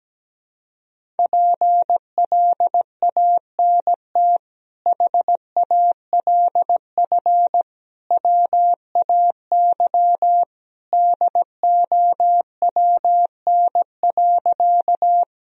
Morse code, heard as a keyed tone, about 17 wpm, 700 Hz. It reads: PLANT HALF WAY DOWN.
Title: Sentences from Top 1000 Words - Review Part 1 - 17wpm